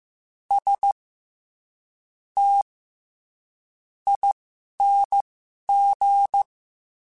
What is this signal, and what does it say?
Morse code, heard as a keyed tone, about 15 wpm, spaced as for 7 wpm, 800 Hz. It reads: STING
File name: Special INGs-mp3s / Special INGs-mp3s / SlowToFaster ing @10